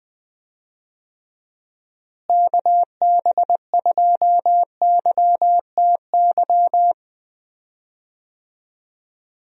Morse code, heard as a keyed tone, about 20 wpm, 700 Hz. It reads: KB2YTY
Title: USA Call Signs - 6 Characters - Repeated 3x - 20wpm